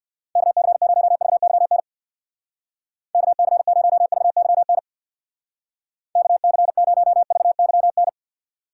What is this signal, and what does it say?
Morse code, heard as a keyed tone, about 54 wpm, 700 Hz. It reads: KC0VYN KC0VYN KC0VYN
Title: Speed Racing - USA 2x3 Call Signs 50wpm